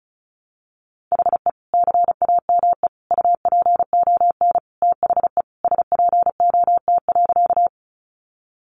Morse code, heard as a keyed tone, about 34 wpm, 700 Hz. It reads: HE CAME UPON THE SPOT.
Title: Sentences from Top 1100 Words 35wpm